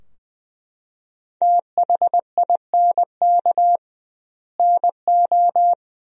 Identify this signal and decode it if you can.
Morse code, heard as a keyed tone, about 20 wpm, 700 Hz. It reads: THINK NO
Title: 2x Word-Spacing - Sentences from Top 200 Words - 20wpm